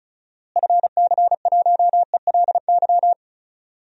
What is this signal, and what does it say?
Morse code, heard as a keyed tone, about 34 wpm, 700 Hz. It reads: FC1ELY
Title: DXCC-100 Call Signs - 6 Characters - Repeated 2x - 35wpm